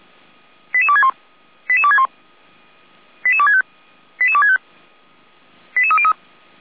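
Five-tone selective call (ZVEI), heard as quick runs of five tones, tones of about 70 ms. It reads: 88261 88261 88265 88265 88303
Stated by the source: ZVEI five-tone selective call received by SDR, from the Feuerwehr Orlamünde (feuerwehr-orlamuende) fire brigade server.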